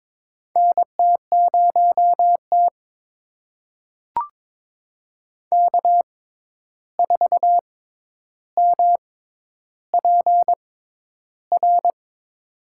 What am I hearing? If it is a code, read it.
Morse code, heard as a keyed tone, about 22 wpm, spaced as for 11 wpm, 700 Hz. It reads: NT0T K4MPR
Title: USA Call Signs - 6x Character Spacing - 22wpm